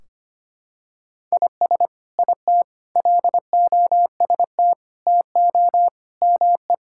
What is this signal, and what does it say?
Morse code, heard as a keyed tone, about 25 wpm, 700 Hz. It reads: IS IT LOST TO ME
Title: Sentences from Top 1600 Words 25wpm